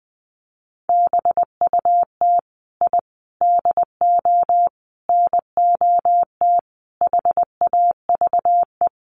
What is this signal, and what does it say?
Morse code, heard as a keyed tone, about 20 wpm, 700 Hz. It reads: BUT I DO NOT HAVE
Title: Sets of 5 Words Encore 20wpm